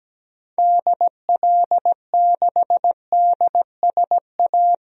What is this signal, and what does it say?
Morse code, heard as a keyed tone, about 17 wpm, 700 Hz. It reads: DL6DSA
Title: DXCC-100 Call Signs - 6 Characters - Repeated 2x - 17wpm